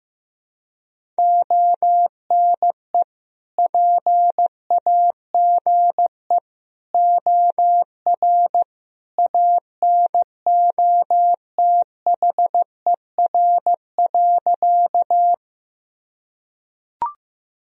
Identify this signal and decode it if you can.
Morse code, heard as a keyed tone, about 15 wpm, 700 Hz. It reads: ONE PAGE OR ANOTHER.
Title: Sentences from Top 1000 Words - Review Part 1 - 15wpm